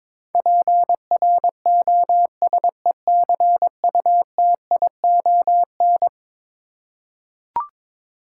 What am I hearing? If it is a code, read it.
Morse code, heard as a keyed tone, about 22 wpm, 700 Hz. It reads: PROSECUTION
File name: practice / 22wpm / Eleven Letter Words - 22wpm